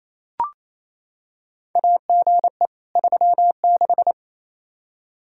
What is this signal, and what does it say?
Morse code, heard as a keyed tone, about 28 wpm, 700 Hz. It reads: AGE 36
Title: Practice QSO - Exchange 3 - More Info - 28wpm